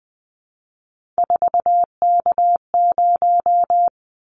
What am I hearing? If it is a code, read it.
Morse code, heard as a keyed tone, about 19 wpm, 700 Hz. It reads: 4K0
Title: Callsign Prefixes for Top 100 DXCC - 20wpm